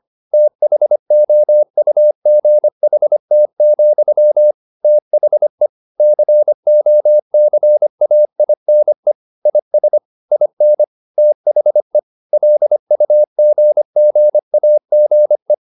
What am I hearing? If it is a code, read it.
Morse code, heard as a keyed tone, about 25 wpm, 600 Hz. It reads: THOUGHT, THE COCAINE IS IN THE LUGGAGE
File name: Never 0025wpm